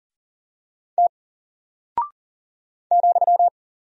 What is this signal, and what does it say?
Morse code, heard as a keyed tone, about 38 wpm, 700 Hz. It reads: T ,
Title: Single Character (Letter:Number:Punctuation:Prosign) 40wpm